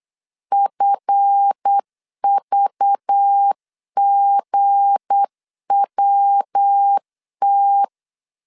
Morse code, heard as a keyed tone, about 8 wpm, 800 Hz. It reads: FVGWT
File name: LEZIONI / 7 GF9Z / 7 GF 8wpm 3108